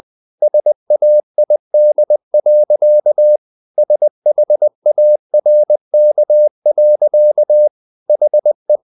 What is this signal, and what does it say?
Morse code, heard as a keyed tone, about 20 wpm, 600 Hz. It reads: SAID. SHARK. HE
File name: the old man and the sea part 2 20wpm